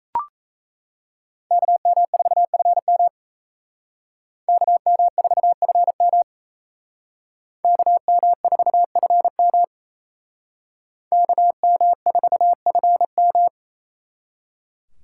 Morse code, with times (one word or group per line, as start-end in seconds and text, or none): 1.5-3.1 KM4FM
4.5-6.2 KM4FM
7.6-9.6 KM4FM
11.1-13.5 KM4FM